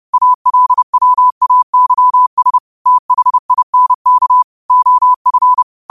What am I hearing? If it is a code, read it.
Morse code, heard as a keyed tone, about 30 wpm, 1000 Hz. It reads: ALWAYS THINK OF
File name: qotdfiles / 30wpm